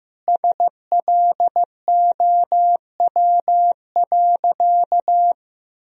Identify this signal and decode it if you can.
Morse code, heard as a keyed tone, about 15 wpm, 700 Hz. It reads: SLOW.
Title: Sentences from Top 1000 Words - Review Part 2 - 15wpm